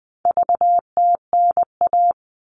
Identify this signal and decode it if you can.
Morse code, heard as a keyed tone, about 20 wpm, 700 Hz. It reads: VTNA
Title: All Cut Numbers - 4-Digit 20wpm